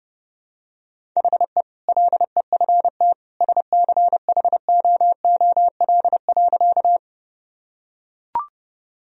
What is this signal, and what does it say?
Morse code, heard as a keyed tone, about 30 wpm, 700 Hz. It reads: HE LEFT SCHOOL.
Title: Sentences from Top 1000 Words - Review Part 1 - 30wpm